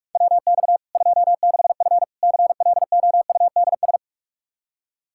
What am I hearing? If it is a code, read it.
Morse code, heard as a keyed tone, about 41 wpm, 700 Hz. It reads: WX 26F CLOUDS
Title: Practice QSO - Exchange 2 - Info - 45wpm